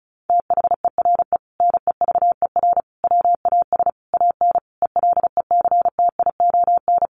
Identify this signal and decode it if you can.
Morse code, heard as a keyed tone, about 34 wpm, 700 Hz. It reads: THERE NEVER WAS AN ELECTION
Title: Sentences from Top 1500 Words 35wpm